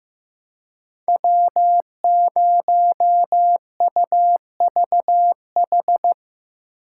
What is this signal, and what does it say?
Morse code, heard as a keyed tone, about 17 wpm, 700 Hz. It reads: W0UVH
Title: Speed Racing - USA 1x3 Call Signs 15wpm